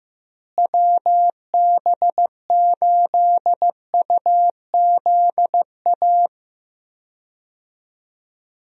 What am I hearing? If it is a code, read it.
Morse code, heard as a keyed tone, about 15 wpm, 700 Hz. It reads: WB8UZA
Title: USA Call Signs - 6 Characters - Repeated 2x - 15wpm